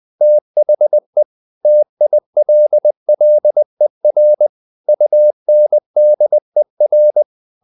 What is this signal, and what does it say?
Morse code, heard as a keyed tone, about 20 wpm, 600 Hz. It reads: THE TILLER UNDER
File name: the old man and the sea part 2 20wpm